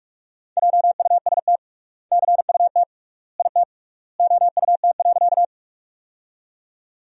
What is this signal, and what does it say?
Morse code, heard as a keyed tone, about 46 wpm, 700 Hz. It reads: JUST CUT IT OUT.
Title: 3x Word-Spacing - Sentences from Top 300 Words - 45wpm